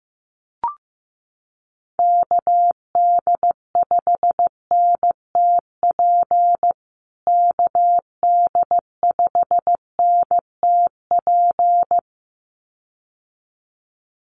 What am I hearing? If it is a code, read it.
Morse code, heard as a keyed tone, about 15 wpm, 700 Hz. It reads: KD5NTP KD5NTP
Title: USA Call Signs Repeated 2x 15wpm